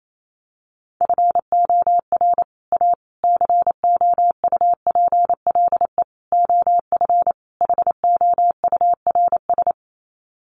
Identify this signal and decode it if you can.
Morse code, heard as a keyed tone, about 28 wpm, 700 Hz. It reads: FOR A COUPLE OF HOURS
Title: Sets of 5 Words Encore2 28wpm